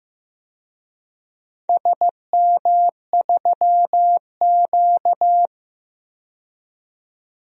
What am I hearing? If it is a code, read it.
Morse code, heard as a keyed tone, about 15 wpm, 700 Hz. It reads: SM3Q
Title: DXCC-100 Call Signs - 4 Characters - 15wpm